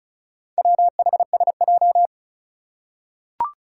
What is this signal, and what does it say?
Morse code, heard as a keyed tone, about 36 wpm, 700 Hz. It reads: WHSJ
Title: Four Challenging Letter-Number Combinations 35wpm